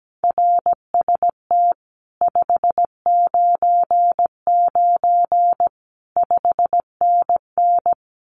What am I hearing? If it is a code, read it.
Morse code, heard as a keyed tone, about 17 wpm, 700 Hz. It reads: RST 599 5NN
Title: Common QSO Phrases Encore 17wpm